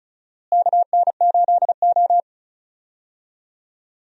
Morse code, heard as a keyed tone, about 36 wpm, 700 Hz. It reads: KN8O
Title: USA Call Signs - 4 Characters - Repeated 2x - 35wpm